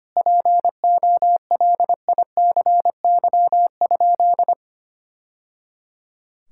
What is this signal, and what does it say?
Morse code, heard as a keyed tone, about 25 wpm, 700 Hz. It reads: POLICY?
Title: Sentences from Top 1000 Words - Review Part 5 - 25wpm